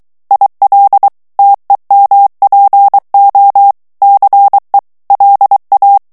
Morse code, heard as a keyed tone, about 23 wpm, 800 Hz. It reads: IL TEMPO CE LA
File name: morse mp3 / ITALIEN texte clair 1 vitesse 22 mots minute